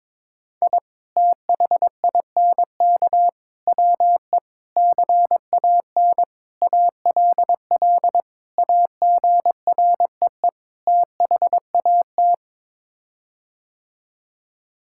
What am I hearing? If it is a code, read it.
Morse code, heard as a keyed tone, about 22 wpm, 700 Hz. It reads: I THINK WE CAN ALL AGREE THAT
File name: practice / 22wpm / Sets of 7 Words - 22wpm